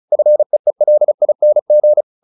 Morse code, heard as a keyed tone, about 34 wpm, 600 Hz. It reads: FEELING
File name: the old man and the sea / the old man and the sea 1 35wpm